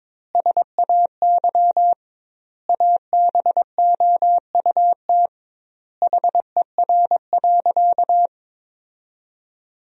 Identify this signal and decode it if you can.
Morse code, heard as a keyed tone, about 22 wpm, 700 Hz. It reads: SAY ABOUT HER.
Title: 2x Word-Spacing - Sentences from Top 200 Words - 22wpm